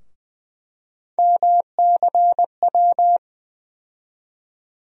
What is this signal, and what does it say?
Morse code, heard as a keyed tone, about 20 wpm, 700 Hz. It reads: MCW
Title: USA Call Sign Postfixes - 3 Letters 20wpm